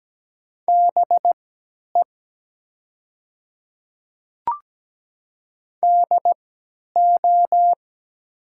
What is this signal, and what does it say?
Morse code, heard as a keyed tone, about 17 wpm, spaced as for 10 wpm, 700 Hz. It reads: BE DO
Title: 3x Character Spacing - Top 200 Words - 17wpm